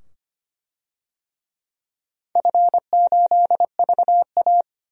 Morse code, heard as a keyed tone, about 25 wpm, 700 Hz. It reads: F8VA
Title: DXCC-100 Call Signs - 4 Characters - 25wpm